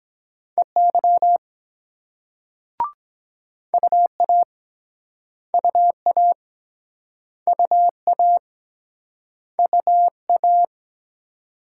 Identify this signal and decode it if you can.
Morse code, heard as a keyed tone, about 24 wpm, 700 Hz. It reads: EY UA UA UA UA